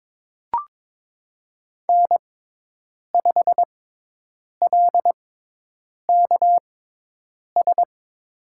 Morse code, heard as a keyed tone, about 22 wpm, spaced as for 8 wpm, 700 Hz. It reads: N5LKS
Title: USA Call Signs - 5 Characters - 6x Character Spacing - 22wpm